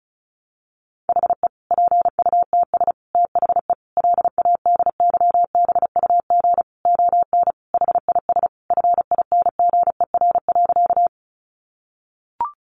Morse code, heard as a keyed tone, about 34 wpm, 700 Hz. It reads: HE PUTS THE LADYBUG ON HIS FINGER.
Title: Additional Easy Phrases 35wpm